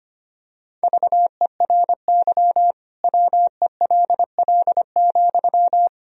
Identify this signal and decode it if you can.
Morse code, heard as a keyed tone, about 25 wpm, 700 Hz. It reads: VERY WELL,
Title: Sentences from Top 1000 Words - Review Part 4 - 25wpm